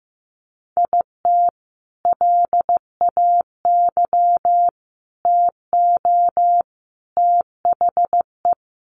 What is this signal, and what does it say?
Morse code, heard as a keyed tone, about 15 wpm, 700 Hz. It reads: IT LAY TO THE
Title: Sentences from Top 1400 Words 15wpm